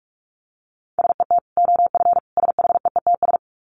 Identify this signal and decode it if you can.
Morse code, heard as a keyed tone, about 44 wpm, 700 Hz. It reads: SET OF SHEETS
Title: Dit Challenge 45wpm